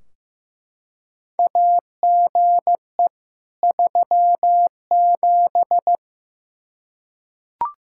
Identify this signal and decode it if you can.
Morse code, heard as a keyed tone, about 15 wpm, 700 Hz. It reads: AGE 37
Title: Practice QSO - Exchange 3 - More Info - 15wpm